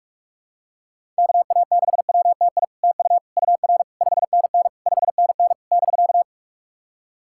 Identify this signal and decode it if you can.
Morse code, heard as a keyed tone, about 46 wpm, 700 Hz. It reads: KA6WTI TU UR 5NN 5NN <BK>
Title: SOTA QSO Practice 45wpm